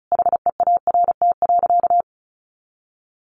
Morse code, heard as a keyed tone, about 34 wpm, 700 Hz. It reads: HEART.
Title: Sentences from Top 1200 Words 35wpm